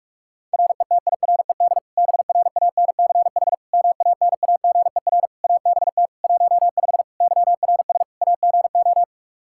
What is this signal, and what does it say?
Morse code, heard as a keyed tone, about 46 wpm, 700 Hz. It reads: RETIRED BRANCH MANAGER ABT 15 YRS AGO
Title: Practice QSO - Exchange 3 - More Info - 45wpm